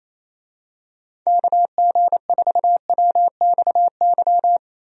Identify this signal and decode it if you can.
Morse code, heard as a keyed tone, about 28 wpm, 700 Hz. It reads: KG4WXY
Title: USA Call Signs - 6 Characters - Repeated 3x - 28wpm